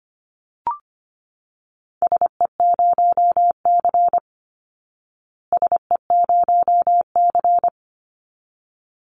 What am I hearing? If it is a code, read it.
Morse code, heard as a keyed tone, about 25 wpm, 700 Hz. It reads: SE0C SE0C
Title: Call Signs Repeated 3x - 25wpm